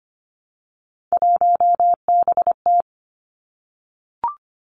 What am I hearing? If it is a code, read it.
Morse code, heard as a keyed tone, about 25 wpm, 700 Hz. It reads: 1BT
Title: Three Challenging Letter-Number Combinations 25wpm